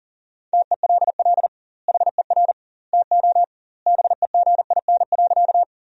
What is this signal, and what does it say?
Morse code, heard as a keyed tone, about 40 wpm, 700 Hz. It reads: TELL HER TO BEGIN.
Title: 2x Word-Spacing - Sentences from Top 300 Words - 40wpm